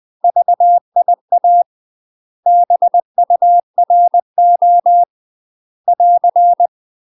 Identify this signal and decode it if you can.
Morse code, heard as a keyed tone, about 20 wpm, 700 Hz. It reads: VIA BURO <AR>
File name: UK MASTER QSOs TRAINING / 20WPM / UK MASTER QSOs 0003